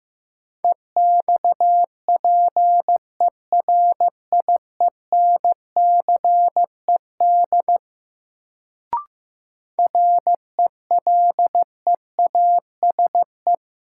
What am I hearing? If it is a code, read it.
Morse code, heard as a keyed tone, about 15 wpm, 700 Hz. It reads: EXPERIENCED RELEASE
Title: Sentences from Top 1100 Words - New Word Drill - 15wpm